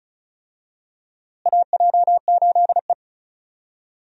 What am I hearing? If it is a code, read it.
Morse code, heard as a keyed tone, about 34 wpm, 700 Hz. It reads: AJ8E
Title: USA Call Signs - 4 Characters - Repeated 2x - 35wpm